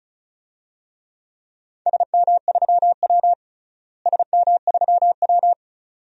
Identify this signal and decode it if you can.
Morse code, heard as a keyed tone, about 36 wpm, 700 Hz. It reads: SM3W SM3W
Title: DXCC-100 Call Signs - 4 Characters - Repeated 2x - 35wpm